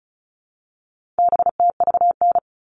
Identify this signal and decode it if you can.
Morse code, heard as a keyed tone, about 34 wpm, 700 Hz. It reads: BTVN
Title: All Cut Numbers - 4-Digit 35wpm